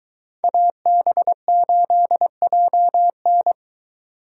Morse code, heard as a keyed tone, about 23 wpm, 700 Hz. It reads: AB8JN